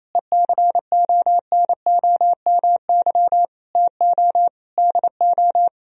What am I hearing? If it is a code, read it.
Morse code, heard as a keyed tone, about 28 wpm, 700 Hz. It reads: ECONOMY TO DO
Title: Sentences from Top 1000 Words - Review Part 3 - 28wpm